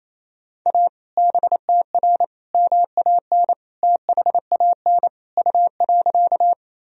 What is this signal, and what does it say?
Morse code, heard as a keyed tone, about 27 wpm, 700 Hz. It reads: A BTR MAN THAN U.
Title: Sentences from Top 400 Words and Abbreviations 28wpm